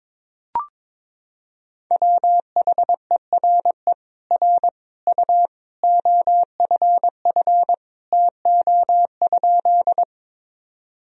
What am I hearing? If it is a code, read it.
Morse code, heard as a keyed tone, about 22 wpm, 700 Hz. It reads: WHERE R U OFF TO?
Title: Sentences from Top 200 Words and Abbreviations 22wpm